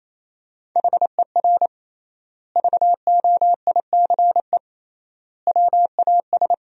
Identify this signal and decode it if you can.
Morse code, heard as a keyed tone, about 28 wpm, 700 Hz. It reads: HER VOICE WAS
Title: 3x Word-Spacing - Sentences from Top 500 Words - 28wpm